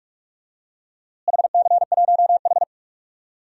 Morse code, heard as a keyed tone, about 43 wpm, 700 Hz. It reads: HC1H
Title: DXCC-100 Call Signs - 4 Characters - Repeated 2x - 45wpm